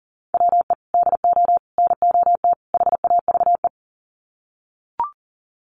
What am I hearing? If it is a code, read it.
Morse code, heard as a keyed tone, about 38 wpm, 700 Hz. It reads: WE DO NOT HAVE
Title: Sets of 4 Words 40wpm